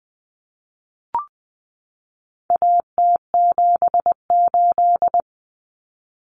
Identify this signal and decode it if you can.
Morse code, heard as a keyed tone, about 20 wpm, 700 Hz. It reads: AT78
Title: Common Cut Numbers - 4-Digit 20wpm